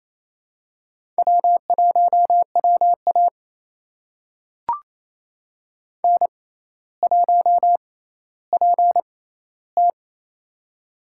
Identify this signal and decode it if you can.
Morse code, heard as a keyed tone, about 28 wpm, spaced as for 14 wpm, 700 Hz. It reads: W1WA N1PT
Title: USA Call Signs - 4 Characters - 6x Character Spacing - 28wpm